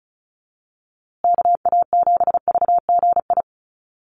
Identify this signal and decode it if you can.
Morse code, heard as a keyed tone, about 34 wpm, 700 Hz. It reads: KA7VGI
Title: USA Call Signs Repeated 2x 35wpm